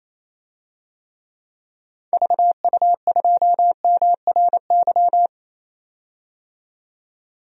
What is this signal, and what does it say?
Morse code, heard as a keyed tone, about 27 wpm, 700 Hz. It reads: VU2MRY
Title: DXCC-100 Call Signs - 6 Characters - 28wpm